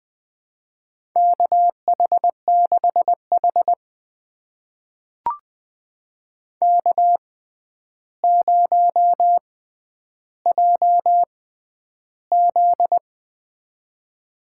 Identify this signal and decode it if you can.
Morse code, heard as a keyed tone, about 21 wpm, spaced as for 10 wpm, 700 Hz. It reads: KH6H K0JZ
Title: USA Call Signs - 4 Characters - 6x Character Spacing - 20wpm